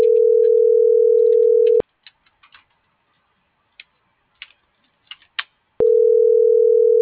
Ringback tone (North America), 440 + 480 Hz.